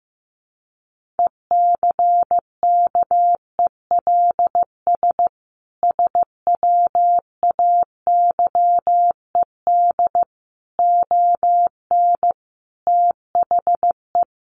Morse code, heard as a keyed tone, about 15 wpm, 700 Hz. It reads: ECKELS SWAYED ON THE
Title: A Sound Of Thunder 15wpm